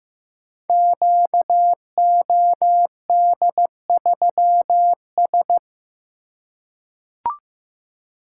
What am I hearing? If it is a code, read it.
Morse code, heard as a keyed tone, about 15 wpm, 700 Hz. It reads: QOD3S
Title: Code Groups - 5 Letter:Numbers - 15wpm